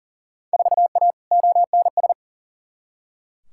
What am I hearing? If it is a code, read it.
Morse code, heard as a keyed tone, about 40 wpm, 700 Hz. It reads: 4A ONS
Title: Field Day Exchanges - 40wpm